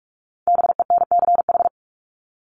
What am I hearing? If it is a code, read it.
Morse code, heard as a keyed tone, about 44 wpm, 700 Hz. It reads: BENCH
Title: Top 1251-1500 Words 45wpm